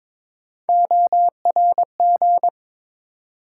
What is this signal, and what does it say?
Morse code, heard as a keyed tone, about 22 wpm, 700 Hz. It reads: ORG
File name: ARRL Sections v2 - 22wpm